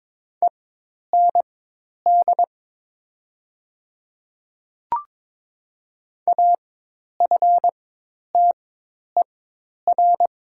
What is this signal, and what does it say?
Morse code, heard as a keyed tone, about 22 wpm, spaced as for 10 wpm, 700 Hz. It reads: END AFTER
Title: 4x Character Spacing - Top 500 Words - 22wpm